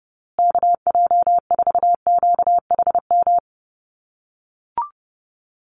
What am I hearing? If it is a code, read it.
Morse code, heard as a keyed tone, about 28 wpm, 700 Hz. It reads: KJ4QHM
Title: USA Call Signs Repeated 2x 30wpm